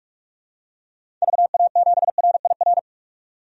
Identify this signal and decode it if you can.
Morse code, heard as a keyed tone, about 46 wpm, 700 Hz. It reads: VA7RIR
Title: DXCC-100 Call Signs - 6 Characters - 45wpm